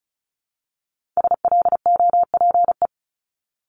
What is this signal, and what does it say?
Morse code, heard as a keyed tone, about 36 wpm, 700 Hz. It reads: SLOPE
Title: Multi-letter Phonemes 35wpm